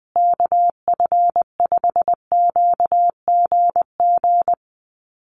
Morse code, heard as a keyed tone, about 20 wpm, 700 Hz. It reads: KF5QGG
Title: USA Call Signs Repeated 2x 20wpm